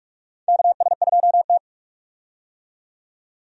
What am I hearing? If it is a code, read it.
Morse code, heard as a keyed tone, about 42 wpm, 700 Hz. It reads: KS2T
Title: USA Call Signs - 4 Characters - 45wpm